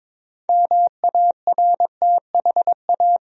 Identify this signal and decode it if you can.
Morse code, heard as a keyed tone, about 22 wpm, 700 Hz. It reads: MARTHA
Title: USA Common Names - 22wpm